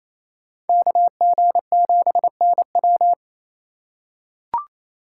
Morse code, heard as a keyed tone, about 28 wpm, 700 Hz. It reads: KG7NW
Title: USA Call Signs - 5 Characters - 4x Character Spacing - 28wpm